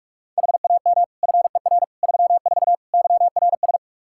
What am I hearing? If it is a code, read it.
Morse code, heard as a keyed tone, about 46 wpm, 700 Hz. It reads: HAM FER 34 YRS
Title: Practice QSO - Exchange 3 - More Info - 45wpm